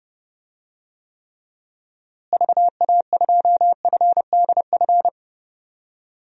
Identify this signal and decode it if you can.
Morse code, heard as a keyed tone, about 30 wpm, 700 Hz. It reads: VA2FDF
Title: DXCC-100 Call Signs - 6 Characters - Repeated 2x - 30wpm